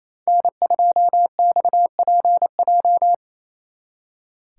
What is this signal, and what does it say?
Morse code, heard as a keyed tone, about 33 wpm, 700 Hz. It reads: N2XPJ
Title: Speed Racing - USA 1x3 Call Signs 28wpm